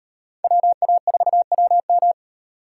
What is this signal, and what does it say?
Morse code, heard as a keyed tone, about 34 wpm, 700 Hz. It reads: WA4WM